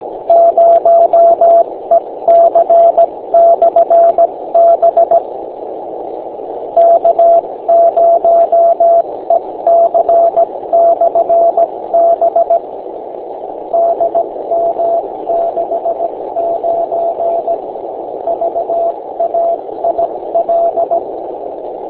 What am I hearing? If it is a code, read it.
Morse code, heard as a keyed tone, about 17 wpm, spaced as for 15 wpm, 680 Hz, with radio noise. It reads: 0EC/B K0EC/B DM69 VAIL
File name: beacons / k0ec-b